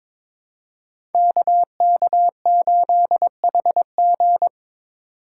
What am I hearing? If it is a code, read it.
Morse code, heard as a keyed tone, about 22 wpm, 700 Hz. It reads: KK8HG
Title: Code Groups - 5 Letter:Numbers - 22wpm